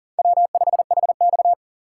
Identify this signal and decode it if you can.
Morse code, heard as a keyed tone, about 40 wpm, 700 Hz. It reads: W5HX
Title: USA Call Signs - 4 Characters - 4x Character Spacing - 40wpm